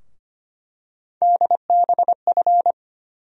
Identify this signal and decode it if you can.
Morse code, heard as a keyed tone, about 25 wpm, 700 Hz. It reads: DBF
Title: USA Call Sign Postfixes - 3 Letters 25wpm